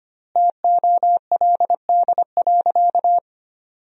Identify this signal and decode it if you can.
Morse code, heard as a keyed tone, about 25 wpm, 700 Hz. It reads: TOLD.
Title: 3x Word-Spacing - Sentences from Top 400 Words - 25wpm